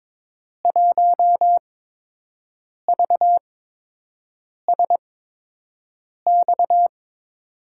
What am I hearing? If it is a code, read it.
Morse code, heard as a keyed tone, about 22 wpm, spaced as for 6 wpm, 700 Hz. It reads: 1VSX
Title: USA Call Signs - 5 Characters - 8x Character Spacing - 22wpm